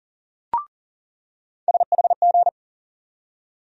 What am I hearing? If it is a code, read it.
Morse code, heard as a keyed tone, about 40 wpm, 700 Hz. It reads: SHG
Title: USA Call Sign Postfixes - 3 Letters 40wpm